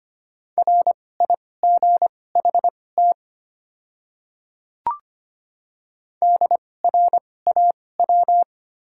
Morse code, heard as a keyed tone, about 25 wpm, spaced as for 18 wpm, 700 Hz. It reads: RIGHT DRAW
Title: 2x Character Spacing - Top 500 Words - 25wpm